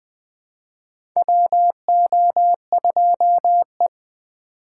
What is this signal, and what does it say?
Morse code, heard as a keyed tone, about 20 wpm, 700 Hz. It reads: WO2E
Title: K1USN SST QSO Practice - 20wpm